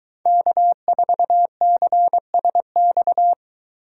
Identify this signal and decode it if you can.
Morse code, heard as a keyed tone, about 22 wpm, 700 Hz. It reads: K4CSX